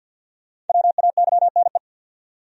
Morse code, heard as a keyed tone, about 52 wpm, 700 Hz. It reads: WAYNE
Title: USA Common Names - Max 5 Letters - 50wpm